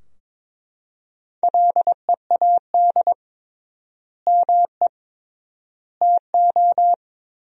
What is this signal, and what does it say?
Morse code, heard as a keyed tone, about 22 wpm, 700 Hz. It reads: LEAD ME TO
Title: 3x Word-Spacing - Sentences from Top 500 Words - 22wpm